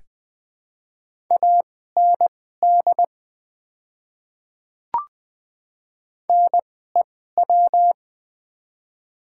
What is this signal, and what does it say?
Morse code, heard as a keyed tone, about 20 wpm, spaced as for 14 wpm, 700 Hz. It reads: AND NEW
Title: 2x Character Spacing - Top 200 Words - 20wpm